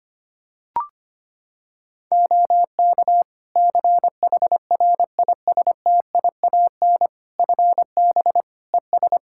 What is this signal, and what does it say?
Morse code, heard as a keyed tone, about 25 wpm, 700 Hz. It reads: OK CHRISTIAN FB ES
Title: Practice QSO - Exchange 2 - Info - 25wpm